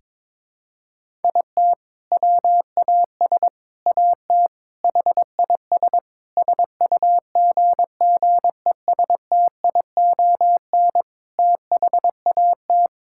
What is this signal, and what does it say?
Morse code, heard as a keyed tone, about 22 wpm, 700 Hz. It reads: IT WAS AT HIS SUGGESTION THAT
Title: Sentences from Top 1600 Words 22wpm